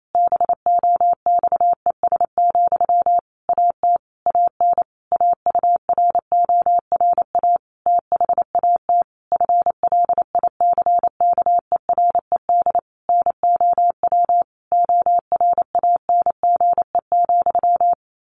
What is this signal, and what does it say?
Morse code, heard as a keyed tone, about 28 wpm, 700 Hz. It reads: BOXES, AT AN AURORA THAT FLICKERED NOW ORANGE,
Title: A Sound Of Thunder 28wpm